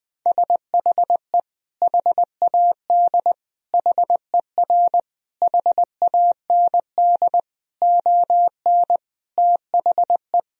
Morse code, heard as a keyed tone, about 21 wpm, 700 Hz. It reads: SHE HAD HER HAND ON THE
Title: Sentences from Top 1000 Words - Review Part 2 - 20wpm